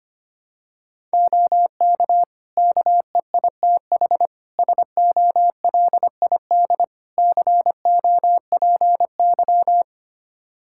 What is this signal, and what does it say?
Morse code, heard as a keyed tone, about 25 wpm, 700 Hz. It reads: OK KEITH SOLID COPY
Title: Practice QSO - Exchange 3 - More Info - 25wpm